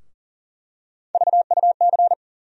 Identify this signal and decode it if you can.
Morse code, heard as a keyed tone, about 40 wpm, 700 Hz. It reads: VUC
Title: USA Call Sign Postfixes - 3 Letters 40wpm